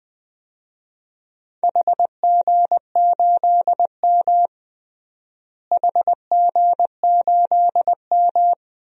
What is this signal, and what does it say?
Morse code, heard as a keyed tone, about 20 wpm, 700 Hz. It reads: HG8M HG8M
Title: DXCC-100 Call Signs - 4 Characters - Repeated 3x - 20wpm